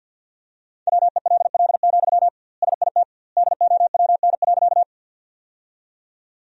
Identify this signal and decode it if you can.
Morse code, heard as a keyed tone, about 50 wpm, 700 Hz. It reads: WELL, SIT DOWN.
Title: 2x Word-Spacing - Sentences from Top 500 Words - 50wpm